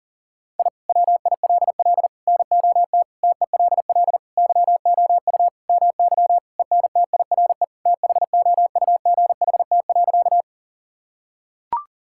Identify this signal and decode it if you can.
Morse code, heard as a keyed tone, about 40 wpm, 700 Hz. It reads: I WILL NOT TELL YOU MY ENTIRE THOUGHT.
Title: Sentences from Top 1000 Words - Review Part 5 - 40wpm